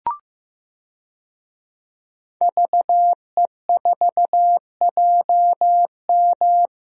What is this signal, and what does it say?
Morse code, heard as a keyed tone, about 15 wpm, 700 Hz. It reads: VE4JM
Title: DXCC-100 Call Signs - 5 Characters - Repeated 3x - 15wpm